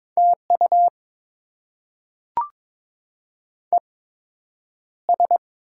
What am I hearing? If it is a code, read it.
Morse code, heard as a keyed tone, about 22 wpm, spaced as for 9 wpm, 700 Hz. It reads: TU ES
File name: Top 20 QSO Elements - 8x Character Spacing - 22wpm